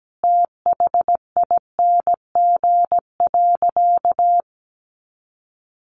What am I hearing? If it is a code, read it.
Morse code, heard as a keyed tone, about 17 wpm, 700 Hz. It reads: THING.
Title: Sentences from Top 1500 Words 17wpm